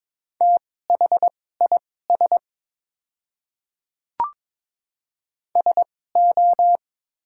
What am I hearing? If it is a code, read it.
Morse code, heard as a keyed tone, about 22 wpm, spaced as for 16 wpm, 700 Hz. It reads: THIS SO
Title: 2x Character Spacing - Top 200 Words - 22wpm